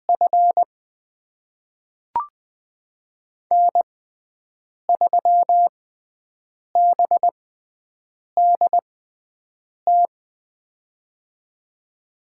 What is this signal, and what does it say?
Morse code, heard as a keyed tone, about 20 wpm, spaced as for 7 wpm, 700 Hz. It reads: F N3BDT
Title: USA Call Signs - 5 Characters - 6x Character Spacing - 20wpm